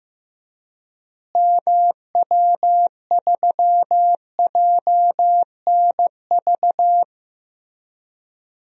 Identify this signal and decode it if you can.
Morse code, heard as a keyed tone, about 15 wpm, 700 Hz. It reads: MW3JNV
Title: DXCC-100 Call Signs - 6 Characters - Repeated 3x - 15wpm